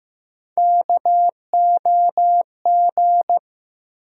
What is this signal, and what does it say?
Morse code, heard as a keyed tone, about 15 wpm, 700 Hz. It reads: KOG
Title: USA Call Sign Postfixes - 3 Letters 15wpm